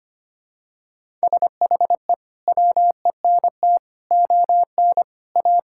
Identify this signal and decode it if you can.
Morse code, heard as a keyed tone, about 25 wpm, 700 Hz. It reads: SHE WENT ON A
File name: practice / 25wpm / Sentences with 2-Word Phrases - 25wpm